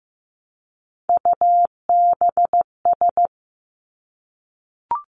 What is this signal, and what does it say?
Morse code, heard as a keyed tone, about 15 wpm, 700 Hz. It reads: UBS
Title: Three Challenging Letter-Number Combinations 15wpm